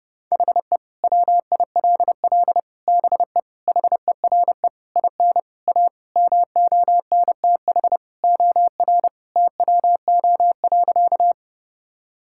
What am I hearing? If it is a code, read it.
Morse code, heard as a keyed tone, about 32 wpm, 700 Hz. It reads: HE WILL BE HERE IN A MONTH OR TWO.
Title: Sentences from Top 1000 Words - Review Part 5 - 30wpm